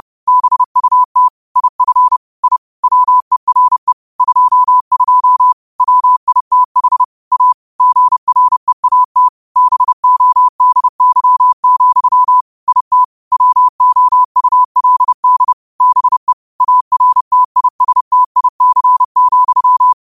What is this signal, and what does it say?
Morse code, heard as a keyed tone, about 30 wpm, 1000 Hz. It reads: BUT IF I WERE 22 WITH A GREAT BODY, IT WOULD BE ARTISTIC,
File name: qotdfiles / 30wpm